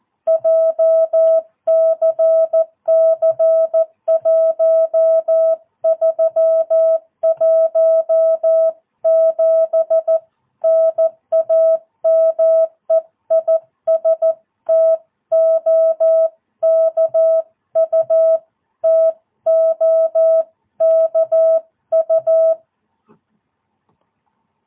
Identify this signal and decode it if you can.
Morse code, heard as a keyed tone, about 14 wpm, 660 Hz. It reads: JCC 1317 NAME IS TOKU TOKU
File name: jh8cbh / 2022cw